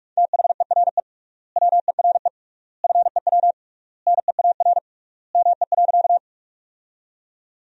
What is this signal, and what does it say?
Morse code, heard as a keyed tone, about 43 wpm, 700 Hz. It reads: THERE WERE FEW NEAR ME.
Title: 3x Word-Spacing - Sentences from Top 300 Words - 45wpm